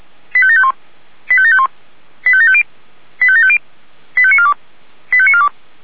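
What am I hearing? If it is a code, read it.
Five-tone selective call (ZVEI), heard as quick runs of five tones, tones of about 70 ms. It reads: 85751 85751 85750 85750 85942 85942
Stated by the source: ZVEI five-tone selective call 85729, received by SDR.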